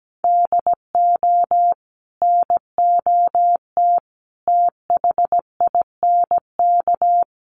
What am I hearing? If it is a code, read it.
Morse code, heard as a keyed tone, about 17 wpm, 700 Hz. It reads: DO NOT THINK
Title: Sets of 4 Words 17wpm